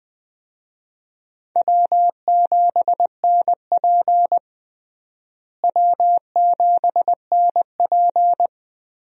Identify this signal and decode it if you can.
Morse code, heard as a keyed tone, about 20 wpm, 700 Hz. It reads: W7NP W7NP
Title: USA Call Signs - 4 Characters - Repeated 3x - 20wpm